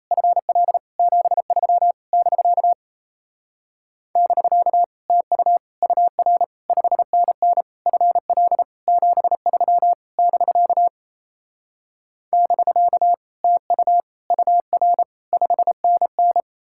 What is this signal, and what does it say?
Morse code, heard as a keyed tone, about 38 wpm, 700 Hz. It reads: FL 73 <BK> <BK> TU UR 5NN FL 73 <BK> <BK> TU UR 5NN